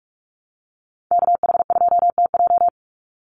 Morse code, heard as a keyed tone, about 44 wpm, 700 Hz. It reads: KH2TJ
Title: NA SOTA Chaser Callsigns 45wpm